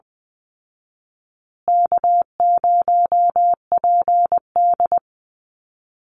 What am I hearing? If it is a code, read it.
Morse code, heard as a keyed tone, about 20 wpm, 700 Hz. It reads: K0PD
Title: Easy Call Signs Repeated 3x - 20wpm